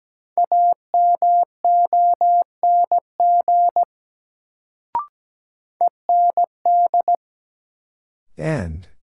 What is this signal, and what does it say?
Morse code, heard as a keyed tone, about 17 wpm, 700 Hz. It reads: AMONG END
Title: Sets of 2 Words - New Word Drill - 17wpm